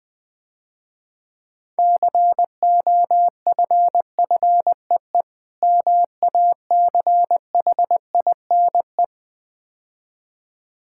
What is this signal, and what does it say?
Morse code, heard as a keyed tone, about 20 wpm, 700 Hz. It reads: COFFEE MACHINE
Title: Difficult Words - Speed-Racing with Hints - 20wpm